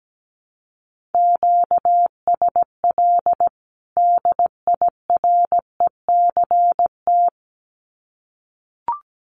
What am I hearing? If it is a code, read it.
Morse code, heard as a keyed tone, about 17 wpm, 700 Hz. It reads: QSL DIRECT
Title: Common QSO Phrases 17wpm